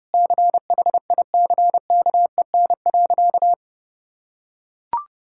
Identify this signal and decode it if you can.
Morse code, heard as a keyed tone, about 32 wpm, 700 Hz. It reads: CHICKEN.
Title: Sentences from Top 1000 Words - Review Part 5 - 30wpm